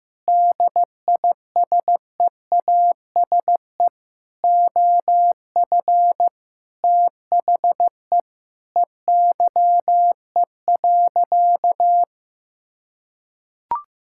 Morse code, heard as a keyed tone, about 15 wpm, 700 Hz. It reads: DISEASE OF THE EYE.
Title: Sentences from Top 1000 Words - Review Part 6 - 15wpm